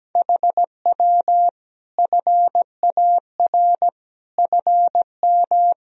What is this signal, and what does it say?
Morse code, heard as a keyed tone, about 17 wpm, 700 Hz. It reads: HW FAR FM
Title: Sentences from Top 400 Words and Abbreviations 17wpm